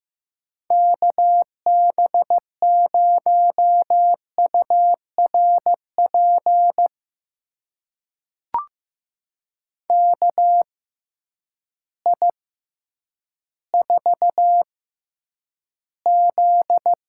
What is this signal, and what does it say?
Morse code, heard as a keyed tone, about 15 wpm, spaced as for 9 wpm, 700 Hz. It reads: KB0URP KI4Z
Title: USA Call Signs - 6 Characters - 6x Character Spacing - 15wpm